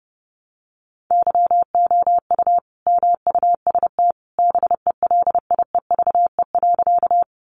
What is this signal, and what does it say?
Morse code, heard as a keyed tone, about 30 wpm, 700 Hz. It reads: YOU MUST BELIEVE.
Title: Sentences from Top 1000 Words 30wpm